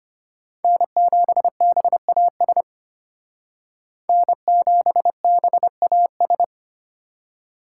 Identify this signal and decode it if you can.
Morse code, heard as a keyed tone, about 30 wpm, 700 Hz. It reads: N7BAS N7BAS
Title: Speed Racing - USA 1x3 Call Signs 25wpm